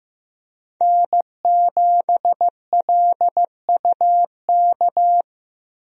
Morse code, heard as a keyed tone, about 15 wpm, 700 Hz. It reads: N7LUK
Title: USA Call Signs - 5 Characters - Repeated 3x - 15wpm